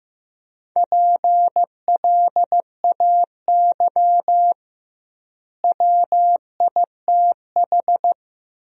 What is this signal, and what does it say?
Morse code, heard as a keyed tone, about 15 wpm, 700 Hz. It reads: PLAY WITH
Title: 2x Word-Spacing - Sentences from Top 200 Words - 15wpm